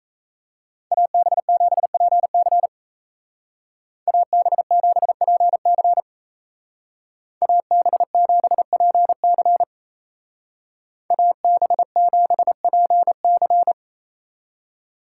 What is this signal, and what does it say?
Morse code, heard as a keyed tone, about 34 wpm, 700 Hz. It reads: AB7PC AB7PC AB7PC AB7PC
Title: Speed Racing - USA 2x2 Call Signs 28wpm